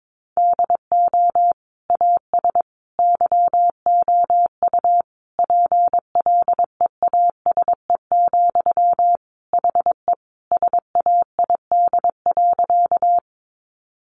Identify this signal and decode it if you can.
Morse code, heard as a keyed tone, about 22 wpm, 700 Hz. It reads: DO AS YOU PLEASE, HE SAID.
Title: Sentences from Top 1500 Words 22wpm